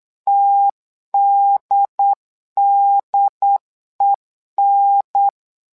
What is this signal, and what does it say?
Morse code, heard as a keyed tone, about 9 wpm, 800 Hz. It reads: TDDEN
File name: LEZIONI / 2 UHDN / 2 UHDN 8wpm2232